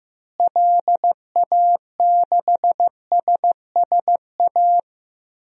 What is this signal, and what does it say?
Morse code, heard as a keyed tone, about 15 wpm, 700 Hz. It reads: LA6SSA
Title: DXCC-100 Call Signs - 6 Characters - Repeated 2x - 15wpm